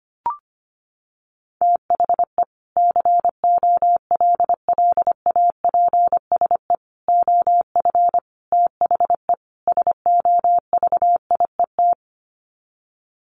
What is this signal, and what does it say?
Morse code, heard as a keyed tone, about 25 wpm, 700 Hz. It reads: THE COLLAPSE OF THE SOVIET
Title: Sets of 5 Words Encore 25wpm